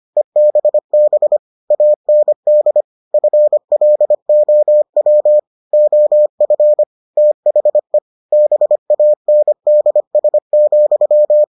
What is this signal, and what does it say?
Morse code, heard as a keyed tone, about 25 wpm, 600 Hz. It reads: EBB AND FLOW OF THE BANDS,